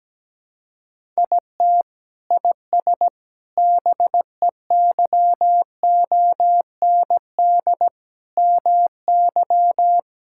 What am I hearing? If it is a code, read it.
Morse code, heard as a keyed tone, about 17 wpm, 700 Hz. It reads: IT IS BEYOND MY
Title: Sentences from Top 1000 Words - Review Part 6 - 17wpm